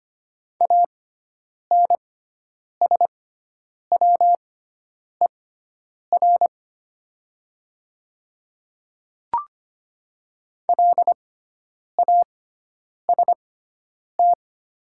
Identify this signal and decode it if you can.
Morse code, heard as a keyed tone, about 25 wpm, spaced as for 9 wpm, 700 Hz. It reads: ANSWER LAST
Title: 6x Character Spacing - Top 300 Words - 25wpm